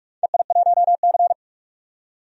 Morse code, heard as a keyed tone, about 41 wpm, 700 Hz. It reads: EI1C
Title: DXCC-100 Call Signs - 4 Characters - 45wpm